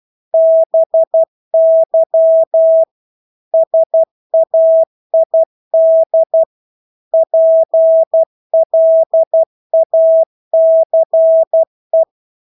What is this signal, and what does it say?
Morse code, heard as a keyed tone, about 12 wpm, 650 Hz. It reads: BY SAID PLACE
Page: Engelske ord 12 wpm